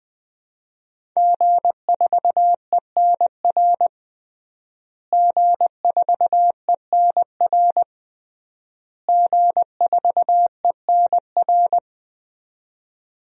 Morse code, heard as a keyed tone, about 20 wpm, 700 Hz. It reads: G4ENR G4ENR G4ENR
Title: DXCC-100 Call Signs - 5 Characters - Repeated 3x - 20wpm